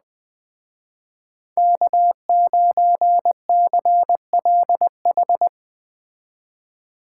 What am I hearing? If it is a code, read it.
Morse code, heard as a keyed tone, about 20 wpm, 700 Hz. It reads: K9CLH
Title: Speed Racing - USA Call Signs 17wpm